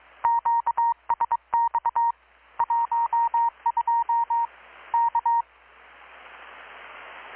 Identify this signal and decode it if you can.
Morse code, heard as a keyed tone, about 21 wpm, 930 Hz. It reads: QSX 12 K
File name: blog / swl / 6-abril-2019-12695-USB-KPS